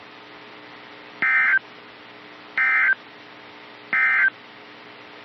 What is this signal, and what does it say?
Emergency Alert System (SAME) tones, heard as harsh buzzing bursts. It is heard from 1.1 to 4.4 s.
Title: NOAA Weather Radio - KEC55